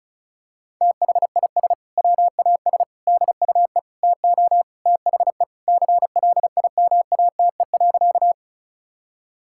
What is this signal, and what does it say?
Morse code, heard as a keyed tone, about 34 wpm, 700 Hz. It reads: THIS WAS DUE TO THE CLIMATE.
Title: Sentences from Top 1600 Words 35wpm